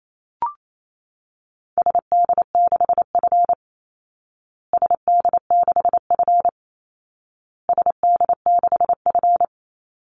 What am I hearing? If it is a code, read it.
Morse code, heard as a keyed tone, about 28 wpm, 700 Hz. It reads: SD6F SD6F SD6F
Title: Call Signs Repeated 2x - 28wpm